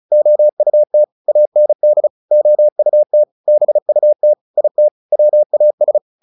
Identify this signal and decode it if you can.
Morse code, heard as a keyed tone, about 34 wpm, 600 Hz. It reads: OUT AND OUT BUT IT WAS
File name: the old man and the sea 2 35wpm